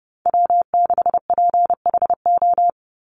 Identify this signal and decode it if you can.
Morse code, heard as a keyed tone, about 30 wpm, 700 Hz. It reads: W6PHO
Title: Call Signs Repeated 2x - 30wpm